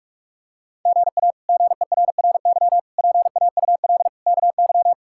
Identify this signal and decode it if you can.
Morse code, heard as a keyed tone, about 42 wpm, 700 Hz. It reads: GA GERRY PAUL KY
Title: K1USN SST QSO Practice - 45wpm